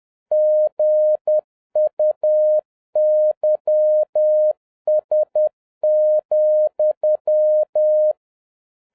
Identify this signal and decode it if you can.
Morse code, heard as a keyed tone, about 10 wpm, 610 Hz. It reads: GUYS,
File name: 2011-07-mp3 / n31 THE 3.11 2 10wpm